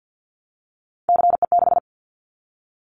Morse code, heard as a keyed tone, about 50 wpm, 690 Hz. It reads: CE6
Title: Callsign Prefixes for Top 100 DXCC - 50wpm